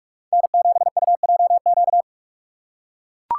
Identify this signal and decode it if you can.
Morse code, heard as a keyed tone, about 41 wpm, 700 Hz. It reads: N7UJQ
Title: USA Call Signs - 5 Characters - 2x Character Spacing - 45wpm